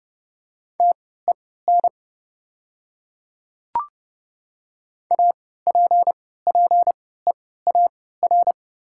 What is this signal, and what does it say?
Morse code, heard as a keyed tone, about 30 wpm, spaced as for 17 wpm, 700 Hz. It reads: TEN APPEAR
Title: 3x Character Spacing - Top 500 Words - 30wpm